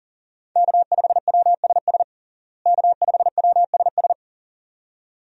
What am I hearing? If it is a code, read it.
Morse code, heard as a keyed tone, about 40 wpm, 700 Hz. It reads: K5WSS K5WSS
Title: USA Call Signs - 5 Characters - Repeated 2x - 40wpm